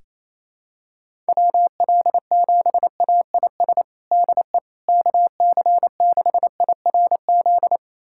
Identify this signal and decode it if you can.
Morse code, heard as a keyed tone, about 28 wpm, 700 Hz. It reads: WL7AIS DE KC6IRZ
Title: USA Callsign DE Callsign 28wpm